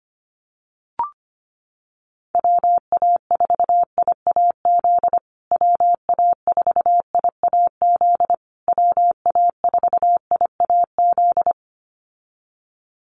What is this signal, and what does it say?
Morse code, heard as a keyed tone, about 25 wpm, 700 Hz. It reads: WA4IAZ WA4IAZ WA4IAZ
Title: USA Call Signs Repeated 3x 25wpm